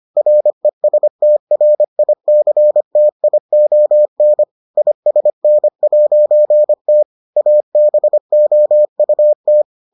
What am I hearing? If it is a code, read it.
Morse code, heard as a keyed tone, about 25 wpm, 600 Hz. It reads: RESTRICTION ISN'T ABOUT